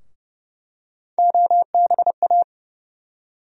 Morse code, heard as a keyed tone, about 31 wpm, 700 Hz. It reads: OBA
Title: USA Call Sign Postfixes - 3 Letters 30wpm